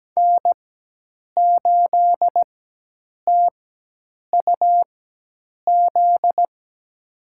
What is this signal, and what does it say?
Morse code, heard as a keyed tone, about 17 wpm, spaced as for 8 wpm, 700 Hz. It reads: N8TUZ
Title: USA Call Signs - 4x Character Spacing - 17wpm